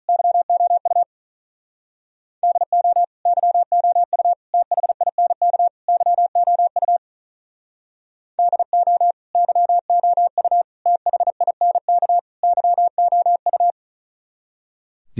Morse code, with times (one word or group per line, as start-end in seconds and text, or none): 0.1-1.0 YOU
2.4-3.0 DO
3.3-4.3 YOU
4.5-5.7 THINK
5.9-7.0 YOU
8.4-9.1 DO
9.3-10.6 YOU
10.9-12.2 THINK
12.4-13.7 YOU